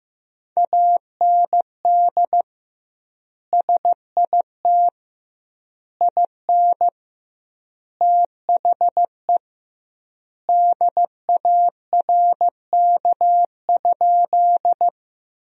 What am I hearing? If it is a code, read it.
Morse code, heard as a keyed tone, about 16 wpm, 700 Hz. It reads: AND SIT IN THE DARK?
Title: 2x Word-Spacing - Sentences from Top 500 Words - 15wpm